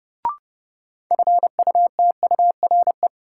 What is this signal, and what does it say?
Morse code, heard as a keyed tone, about 30 wpm, 700 Hz. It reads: FUTURE
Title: Sentences from Top 700 Words - New Word Drill - 30wpm